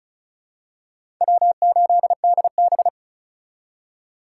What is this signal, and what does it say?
Morse code, heard as a keyed tone, about 36 wpm, 700 Hz. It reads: W8DB
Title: USA Call Signs - 4 Characters - 3x Character Spacing - 35wpm